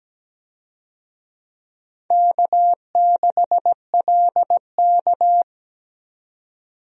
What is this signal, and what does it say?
Morse code, heard as a keyed tone, about 17 wpm, 700 Hz. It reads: K6LK
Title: USA Call Signs - 4 Characters - Repeated 3x - 17wpm